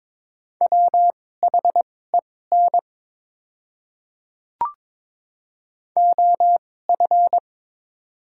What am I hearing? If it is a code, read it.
Morse code, heard as a keyed tone, about 22 wpm, spaced as for 16 wpm, 690 Hz. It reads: WHEN OF
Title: 2x Character Spacing - Top 100 Words - 22wpm